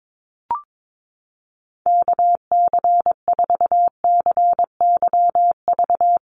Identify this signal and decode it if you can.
Morse code, heard as a keyed tone, about 22 wpm, 700 Hz. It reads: KC4CYV
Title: USA Call Signs 22wpm